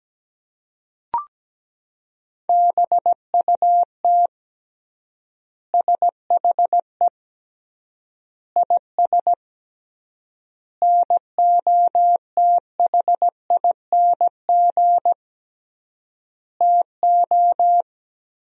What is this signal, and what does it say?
Morse code, heard as a keyed tone, about 17 wpm, 700 Hz. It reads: BUT SHE IS NOTHING TO
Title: 3x Word-Spacing - Sentences from Top 500 Words - 17wpm